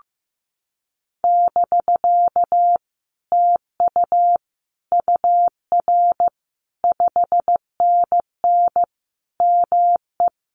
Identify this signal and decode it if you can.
Morse code, heard as a keyed tone, about 15 wpm, 700 Hz. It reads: <BK> TU UR 5NN ME
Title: Common QSO Phrases Encore 15wpm